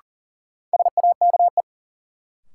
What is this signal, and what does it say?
Morse code, heard as a keyed tone, about 38 wpm, 700 Hz. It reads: SAKE
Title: Sets of 4 Words - New Word Drill - 40wpm